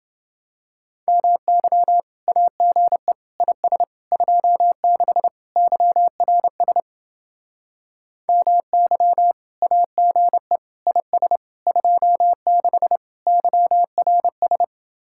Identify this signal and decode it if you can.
Morse code, heard as a keyed tone, about 30 wpm, 700 Hz. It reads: MY AGE IS 26 YRS MY AGE IS 26 YRS